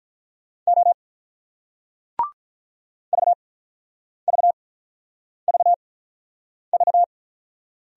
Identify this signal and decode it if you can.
Morse code, heard as a keyed tone, about 48 wpm, spaced as for 9 wpm, 700 Hz. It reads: O VVVV